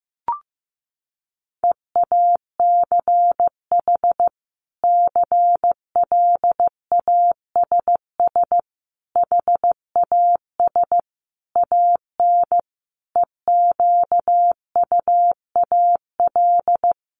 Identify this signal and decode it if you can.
Morse code, heard as a keyed tone, about 15 wpm, 700 Hz. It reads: EACH CLASS HAS AN EQUAL
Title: Sentences from Top 1400 Words 15wpm